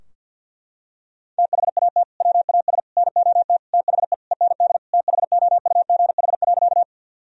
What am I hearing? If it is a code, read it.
Morse code, heard as a keyed tone, about 48 wpm, 700 Hz. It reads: THAT WAS NOT THE END THOUGH.
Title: Sentences from Top 1000 Words - Review Part 2 - 50wpm